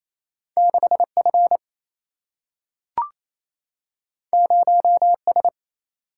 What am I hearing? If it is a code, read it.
Morse code, heard as a keyed tone, about 28 wpm, 700 Hz. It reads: 6 F 0 S
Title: Single-Letter Single-Number 28wpm